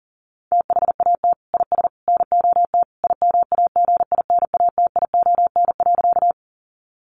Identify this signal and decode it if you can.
Morse code, heard as a keyed tone, about 40 wpm, 700 Hz. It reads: THAT IS NOT IMAGINATION.
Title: Sentences from Top 1300 Words 40wpm